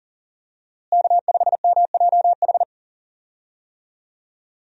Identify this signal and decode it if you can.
Morse code, heard as a keyed tone, about 40 wpm, 700 Hz. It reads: K5MJH
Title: USA Call Signs - 5 Characters - Repeated 2x - 40wpm